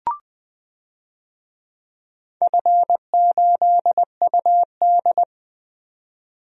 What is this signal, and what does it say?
Morse code, heard as a keyed tone, about 20 wpm, 700 Hz. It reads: F8UD
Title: DXCC-100 Call Signs - 4 Characters - 20wpm